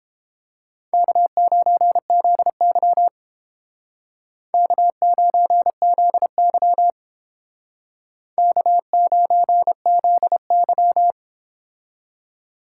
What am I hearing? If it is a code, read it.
Morse code, heard as a keyed tone, about 31 wpm, 700 Hz. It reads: K9ZY K9ZY K9ZY